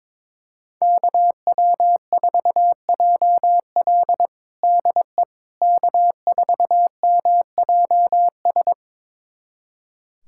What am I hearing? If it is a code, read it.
Morse code, heard as a keyed tone, about 22 wpm, 700 Hz. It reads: KW4JL DE K4MJS
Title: USA Callsign DE Callsign - 5 Characters - 22wpm